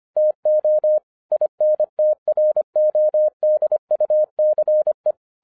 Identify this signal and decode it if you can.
Morse code, heard as a keyed tone, about 24 wpm, 610 Hz. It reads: TO INTRODUCE